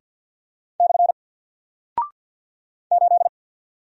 Morse code, heard as a keyed tone, about 48 wpm, 700 Hz. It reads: / 8
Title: Single Character (Letter:Number:Punctuation:Prosign) Alt 50wpm